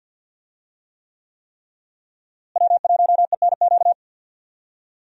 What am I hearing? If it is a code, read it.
Morse code, heard as a keyed tone, about 46 wpm, 700 Hz. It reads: W1ENQ
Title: USA Call Signs - 5 Characters - Repeated 3x - 50wpm